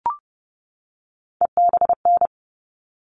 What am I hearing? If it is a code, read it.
Morse code, heard as a keyed tone, about 30 wpm, 700 Hz. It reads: EBN
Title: All Cut Numbers - 3-Digit 30wpm